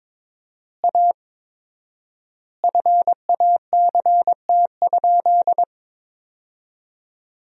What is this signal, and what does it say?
Morse code, heard as a keyed tone, about 21 wpm, 700 Hz. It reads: A FACT?
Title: 4x Word-Spacing - Sentences from Top 500 Words - 22wpm